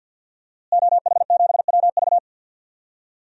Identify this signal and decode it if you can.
Morse code, heard as a keyed tone, about 49 wpm, 700 Hz. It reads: OH7WV
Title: DXCC-100 Call Signs - 5 Characters - Repeated 3x - 50wpm